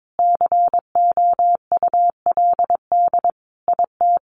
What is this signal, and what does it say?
Morse code, heard as a keyed tone, about 22 wpm, 700 Hz. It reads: COULD IT
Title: Sentences from Top 1300 Words 22wpm